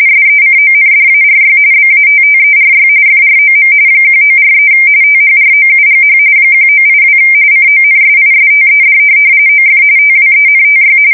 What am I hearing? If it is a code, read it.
RTTY, about 45.45 baud. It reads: SECONDARY BATTERY FROM 1943 THROUGH 1982 CONSISTED OF TWENTY
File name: fsk170 / USS-New-Jersey